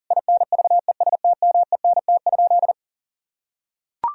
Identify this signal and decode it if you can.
Morse code, heard as a keyed tone, about 40 wpm, 700 Hz. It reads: INVESTMENT?
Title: Sentences from Top 1000 Words - Review Part 4 - 40wpm